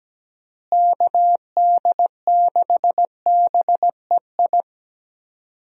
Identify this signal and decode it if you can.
Morse code, heard as a keyed tone, about 17 wpm, 700 Hz. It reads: KD6BEI
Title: USA Call Signs - 6 Characters - Repeated 3x - 17wpm